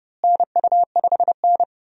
Morse code, heard as a keyed tone, about 30 wpm, 700 Hz. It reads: NU5N
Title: USA Call Signs - 4 Characters - Repeated 3x - 30wpm